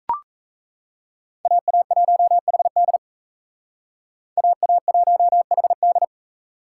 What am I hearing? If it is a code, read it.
Morse code, heard as a keyed tone, about 35 wpm, 700 Hz. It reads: AA1HD AA1HD